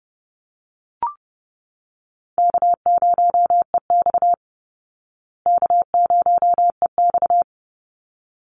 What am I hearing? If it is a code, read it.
Morse code, heard as a keyed tone, about 28 wpm, 700 Hz. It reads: K0EX K0EX
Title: Easy Call Signs Repeated 2x - 30wpm